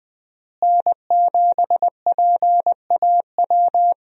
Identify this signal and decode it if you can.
Morse code, heard as a keyed tone, about 20 wpm, 700 Hz. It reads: N7PAW
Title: USA Call Signs - 5 Characters - Repeated 3x - 20wpm